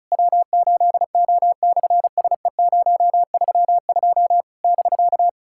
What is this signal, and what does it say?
Morse code, heard as a keyed tone, about 36 wpm, 660 Hz. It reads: W8O/SE032 <BK>
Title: SOTA QSO Practice 35wpm